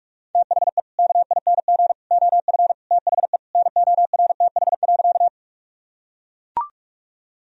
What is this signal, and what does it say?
Morse code, heard as a keyed tone, about 41 wpm, 700 Hz. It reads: THE KING OF THE NORTH.
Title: Sentences from Top 1000 Words - Review Part 1 - 45wpm